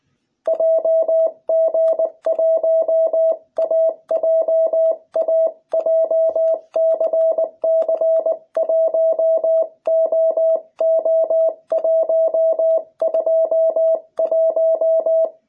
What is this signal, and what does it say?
Morse code, heard as a keyed tone, about 20 wpm, 660 Hz. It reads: JG1AJA JCC1OO121